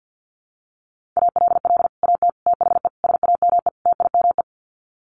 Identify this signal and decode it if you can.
Morse code, heard as a keyed tone, about 50 wpm, 700 Hz. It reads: ALL AT THE SAME TIME
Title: Sets of 5 Words Encore2 50wpm